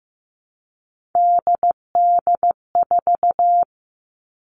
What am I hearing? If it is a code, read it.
Morse code, heard as a keyed tone, about 15 wpm, 700 Hz. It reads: DD4
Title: Three Challenging Letter-Number Combinations 15wpm